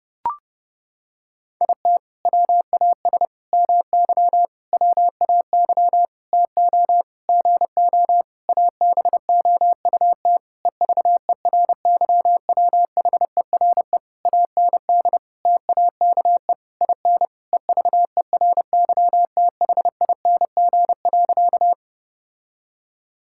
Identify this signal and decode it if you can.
Morse code, heard as a keyed tone, about 32 wpm, 700 Hz. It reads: IT WAS MY WAY TO GO ABOUT EVERYWHERE AND TAKE IN EVERYTHING.
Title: Sentences from Top 1600 Words 30wpm